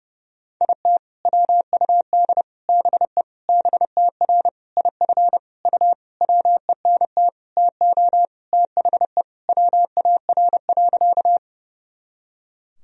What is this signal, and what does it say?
Morse code, heard as a keyed tone, about 30 wpm, 700 Hz. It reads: IT WUD BE BTR IF U WENT TO THE WAR.
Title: Sentences from Top 400 Words and Abbreviations 30wpm